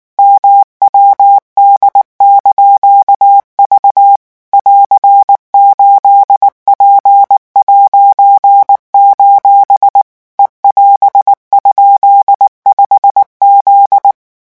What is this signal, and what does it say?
Morse code, heard as a keyed tone, about 19 wpm, 800 Hz. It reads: MWD)V <AR>8P': E<AS>?5Z
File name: morse mp3 / revision19-vitesse 18 mots